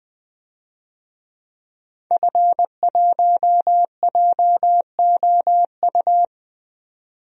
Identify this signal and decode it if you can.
Morse code, heard as a keyed tone, about 20 wpm, 700 Hz. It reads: F1JOU
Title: DXCC-100 Call Signs - 5 Characters - 20wpm